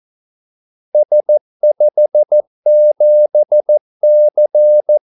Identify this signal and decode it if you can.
Morse code, heard as a keyed tone, about 14 wpm, 600 Hz. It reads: S57C
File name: Callsigns-14wpm